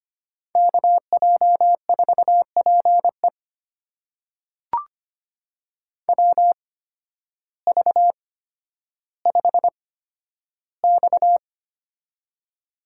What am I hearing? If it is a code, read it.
Morse code, heard as a keyed tone, about 25 wpm, spaced as for 12 wpm, 700 Hz. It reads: KJ4PE WV5X
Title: USA Call Signs - 8x Character Spacing - 25wpm